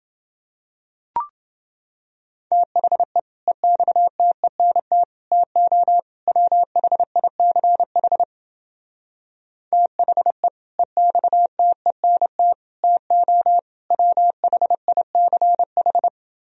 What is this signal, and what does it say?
Morse code, heard as a keyed tone, about 29 wpm, 700 Hz. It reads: THE EXTENT TO WHICH THE EXTENT TO WHICH